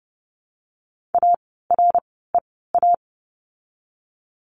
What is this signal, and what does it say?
Morse code, heard as a keyed tone, about 30 wpm, spaced as for 17 wpm, 700 Hz. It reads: AREA
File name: course / 30:10wpm / 108 Lesson 18f - Letter m - Words 30wpm